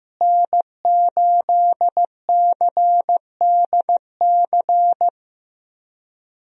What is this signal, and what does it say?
Morse code, heard as a keyed tone, about 15 wpm, 700 Hz. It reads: N8CDC
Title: USA Call Signs - 5 Characters - Repeated 2x - 15wpm